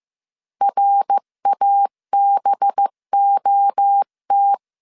Morse code, heard as a keyed tone, about 15 wpm, 800 Hz. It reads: RABOT
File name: CW / DET / 1 9 1500C S B 14wpm 3302